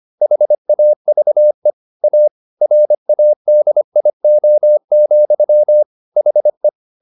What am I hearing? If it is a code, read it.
Morse code, heard as a keyed tone, about 25 wpm, 600 Hz. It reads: HAVE A RADIO, HE